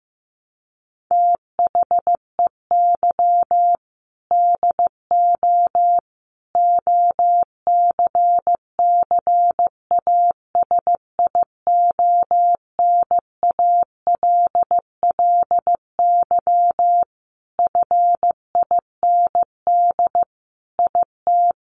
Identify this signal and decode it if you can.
Morse code, heard as a keyed tone, about 14 wpm, 700 Hz. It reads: THEY DO OCCASIONALLY FIND IT
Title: Sentences from Top 1300 Words 15wpm